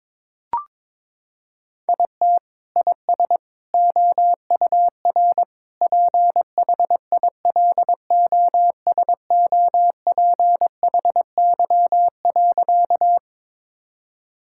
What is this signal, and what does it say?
Morse code, heard as a keyed tone, about 22 wpm, 700 Hz. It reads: IT IS OUR PHILOSOPHY.
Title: Sentences from Top 1000 Words - Review Part 6 - 22wpm